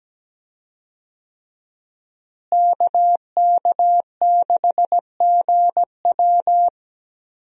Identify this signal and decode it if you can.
Morse code, heard as a keyed tone, about 17 wpm, 700 Hz. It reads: KK6GW
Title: USA Call Signs - 5 Characters - Repeated 3x - 17wpm